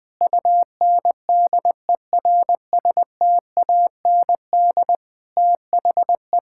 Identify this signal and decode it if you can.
Morse code, heard as a keyed tone, about 20 wpm, 700 Hz. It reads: UNDERSTAND THE
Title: Sentences with 6-Word Phrases - 20wpm